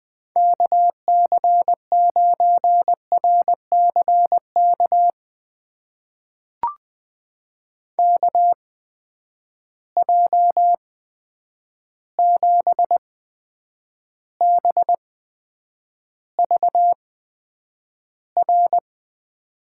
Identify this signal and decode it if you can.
Morse code, heard as a keyed tone, about 20 wpm, spaced as for 9 wpm, 700 Hz. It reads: KC9RCK KJ7BVR
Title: USA Call Signs - 6 Characters - 8x Character Spacing - 20wpm